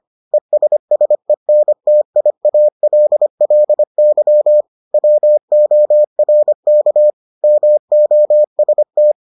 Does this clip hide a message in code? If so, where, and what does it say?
Yes: Morse code from 0.3 to 9.1 s, heard as a keyed tone, about 25 wpm, 600 Hz. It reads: ESSENTIALLY WORK MOST